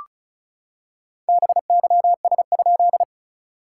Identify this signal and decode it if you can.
Morse code, heard as a keyed tone, about 37 wpm, 700 Hz. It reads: BYS?
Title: SOTA QSO Practice 35wpm